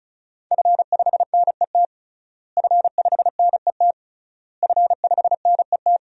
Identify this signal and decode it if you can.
Morse code, heard as a keyed tone, about 36 wpm, 700 Hz. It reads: F5NET F5NET F5NET
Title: DXCC-100 Call Signs - 5 Characters - Repeated 3x - 35wpm